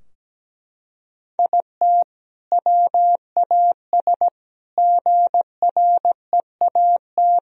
Morse code, heard as a keyed tone, about 17 wpm, 700 Hz. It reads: IT WAS GREAT
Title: Sentences from Top 1000 Words - Review Part 4 - 17wpm